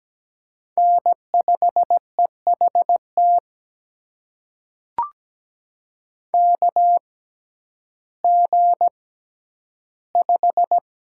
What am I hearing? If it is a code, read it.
Morse code, heard as a keyed tone, about 17 wpm, spaced as for 10 wpm, 700 Hz. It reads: N5EHT KG5
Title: USA Call Signs - 6x Character Spacing - 17wpm